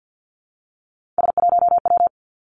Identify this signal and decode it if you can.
Morse code, heard as a keyed tone, about 50 wpm, 700 Hz. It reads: S1W
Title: Three Challenging Letter-Number Combinations 50wpm